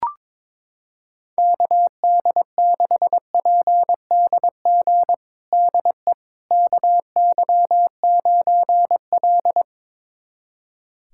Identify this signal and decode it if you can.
Morse code, heard as a keyed tone, about 21 wpm, 700 Hz. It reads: KD6PDG DE KY9L
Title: USA Callsign DE Callsign 22wpm